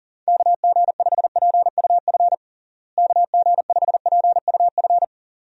Morse code, heard as a keyed tone, about 40 wpm, 700 Hz. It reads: KG5PUF KG5PUF
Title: USA Call Signs - 6 Characters - Repeated 2x - 40wpm